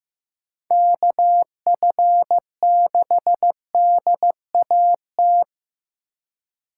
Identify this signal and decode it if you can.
Morse code, heard as a keyed tone, about 15 wpm, 700 Hz. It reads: KF6DAT
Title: USA Call Signs - 6 Characters - 4x Character Spacing - 15wpm